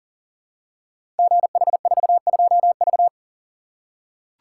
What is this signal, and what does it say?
Morse code, heard as a keyed tone, about 38 wpm, 690 Hz. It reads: GH42V
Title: Code Groups - 5 Letter:Numbers - 40wpm